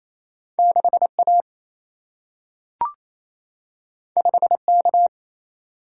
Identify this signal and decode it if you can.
Morse code, heard as a keyed tone, about 28 wpm, 700 Hz. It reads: 6 A 5 K
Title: Single-Letter Single-Number 28wpm